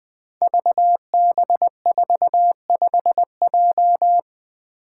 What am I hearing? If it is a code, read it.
Morse code, heard as a keyed tone, about 20 wpm, 700 Hz. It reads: VB45J
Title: Code Groups - 5 Letter:Numbers - 20wpm